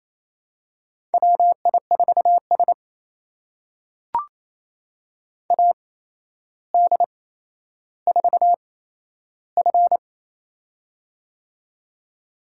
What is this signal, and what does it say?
Morse code, heard as a keyed tone, about 27 wpm, spaced as for 12 wpm, 700 Hz. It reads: WI4S AD4F
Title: USA Call Signs - 4 Characters - 8x Character Spacing - 28wpm